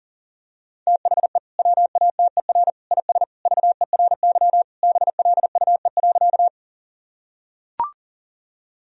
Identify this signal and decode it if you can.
Morse code, heard as a keyed tone, about 40 wpm, 700 Hz. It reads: THE WATER IS VERY BLUE.
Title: Sentences from Top 1000 Words - Review Part 3 - 40wpm